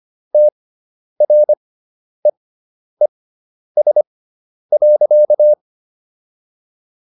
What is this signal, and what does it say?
Morse code, heard as a keyed tone, about 25 wpm, spaced as for 10 wpm, 600 Hz. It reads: TREES.